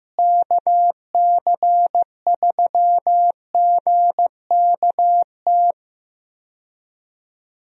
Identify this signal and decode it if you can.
Morse code, heard as a keyed tone, about 15 wpm, 700 Hz. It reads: KC3GKT
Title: USA Call Signs - 6 Characters - Repeated 3x - 15wpm